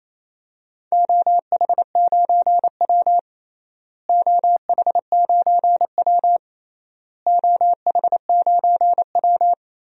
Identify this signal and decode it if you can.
Morse code, heard as a keyed tone, about 28 wpm, 700 Hz. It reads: OH9W OH9W OH9W
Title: DXCC-100 Call Signs - 4 Characters - Repeated 3x - 28wpm